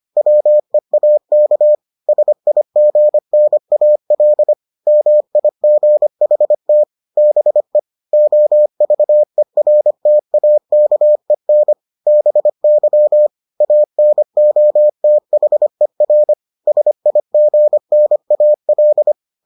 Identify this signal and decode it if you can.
Morse code, heard as a keyed tone, about 25 wpm, 600 Hz. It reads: WEAK SIGNAL MIGHT BE OVERTAKEN BY ANOTHER SIGNAL